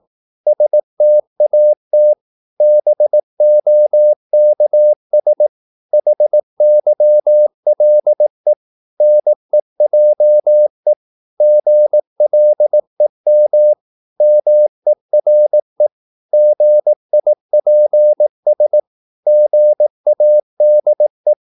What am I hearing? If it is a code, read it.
Morse code, heard as a keyed tone, about 18 wpm, 600 Hz. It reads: STAT BOKS HYLE NEJE GLEM MERE GIPS GADE
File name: Korte ord DK 18wpm